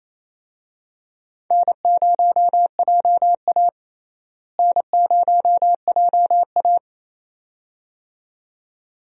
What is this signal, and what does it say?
Morse code, heard as a keyed tone, about 28 wpm, 700 Hz. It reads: N0JA N0JA
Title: USA Call Signs - 4 Characters - Repeated 2x - 28wpm